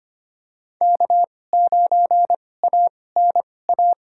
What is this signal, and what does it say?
Morse code, heard as a keyed tone, about 25 wpm, spaced as for 18 wpm, 700 Hz. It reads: K9ANA
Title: USA Call Signs - 5 Characters - 2x Character Spacing - 25wpm